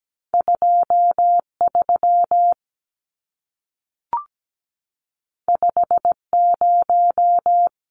Morse code, heard as a keyed tone, about 17 wpm, 700 Hz. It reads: 23 50
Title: Numbers - 2-Digit 17wpm